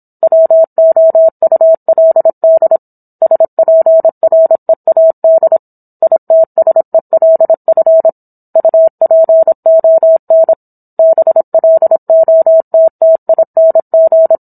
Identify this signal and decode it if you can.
Morse code, heard as a keyed tone, about 27 wpm, 650 Hz. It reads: WOULD SPREAD ITSELF UPON BLOTTING
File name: Book One / 17-Chapter 17 - 26 WPM